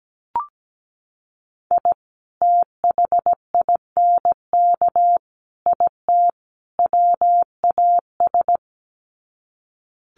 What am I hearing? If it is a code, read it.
Morse code, heard as a keyed tone, about 17 wpm, 700 Hz. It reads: I THINK IT WAS
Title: Sets of 4 Words 17wpm